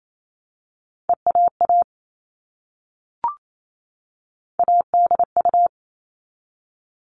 Morse code, heard as a keyed tone, about 28 wpm, 700 Hz. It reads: EAA ADU
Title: All Cut Numbers - 3-Digit 28wpm